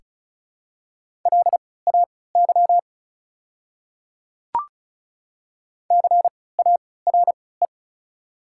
Morse code, heard as a keyed tone, about 36 wpm, spaced as for 20 wpm, 690 Hz. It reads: LAY CARE
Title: 3x Character Spacing - Top 500 Words - 35wpm